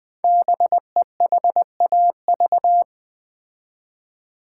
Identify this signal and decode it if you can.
Morse code, heard as a keyed tone, about 20 wpm, 700 Hz. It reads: BEHAV
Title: Build A Word - 5 Letters At A Time - 10-Letter Words - 20wpm